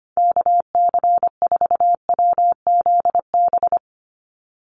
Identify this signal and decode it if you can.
Morse code, heard as a keyed tone, about 25 wpm, 700 Hz. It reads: KC4WZB
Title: NA SOTA Activator Callsigns 25wpm